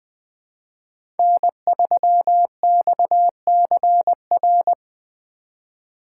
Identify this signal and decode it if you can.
Morse code, heard as a keyed tone, about 20 wpm, 700 Hz. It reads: N3XCR
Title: USA Call Signs - 5 Characters - 2x Character Spacing - 20wpm